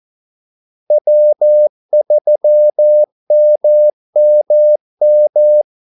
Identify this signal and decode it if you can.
Morse code, heard as a keyed tone, about 14 wpm, 600 Hz. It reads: W3MMM
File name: Callsigns-14wpm